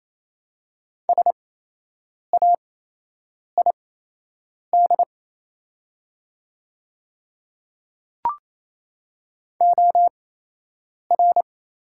Morse code, heard as a keyed tone, about 28 wpm, spaced as for 8 wpm, 700 Hz. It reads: SAID OR
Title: 8x Character Spacing - Top 100 Words - 28wpm